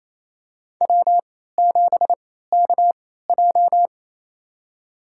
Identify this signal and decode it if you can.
Morse code, heard as a keyed tone, about 28 wpm, spaced as for 16 wpm, 700 Hz. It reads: W7KJ
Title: USA Call Signs - 4 Characters - 3x Character Spacing - 28wpm